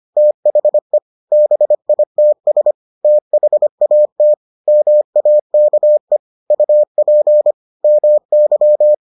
Morse code, heard as a keyed tone, about 25 wpm, 600 Hz. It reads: THE BITS THAT MAKE UP MY